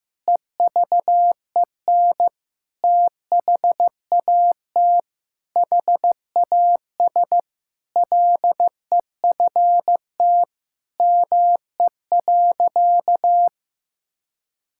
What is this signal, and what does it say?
Morse code, heard as a keyed tone, about 15 wpm, 700 Hz. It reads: EVEN THAT HAS LEFT ME.
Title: Sentences from Top 1600 Words 15wpm